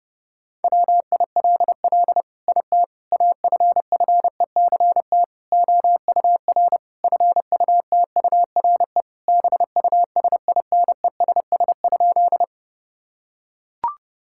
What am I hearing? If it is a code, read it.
Morse code, heard as a keyed tone, about 30 wpm, 700 Hz. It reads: WILL IT AFFECT OUR FUTURE BUSINESS?
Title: Sentences from Top 1000 Words - Review Part 6 - 30wpm